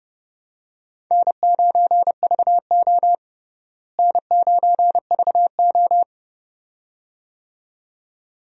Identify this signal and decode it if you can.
Morse code, heard as a keyed tone, about 32 wpm, 700 Hz. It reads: N9VO N9VO
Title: USA Call Signs - 4 Characters - Repeated 2x - 30wpm